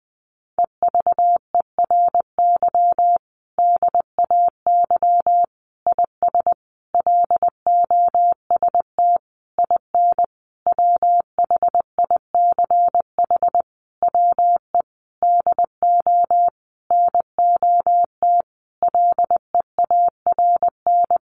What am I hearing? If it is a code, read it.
Morse code, heard as a keyed tone, about 20 wpm, 700 Hz. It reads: EVERY DAY IS LOST IN WHICH WE DO NOT LEARN
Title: Sentences from Top 1400 Words 20wpm